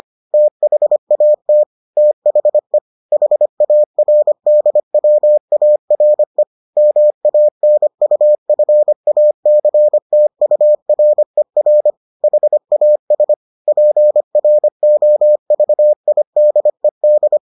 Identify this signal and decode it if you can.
Morse code, heard as a keyed tone, about 25 wpm, 600 Hz. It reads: THAT THE HARDWARE MANUFACTURER HAS PROVIDED